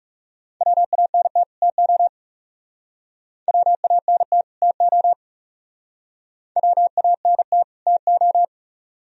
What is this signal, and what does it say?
Morse code, heard as a keyed tone, about 37 wpm, 700 Hz. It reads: WANT TO WANT TO WANT TO
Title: Speed-Racing - Sets of 2 Words 30wpm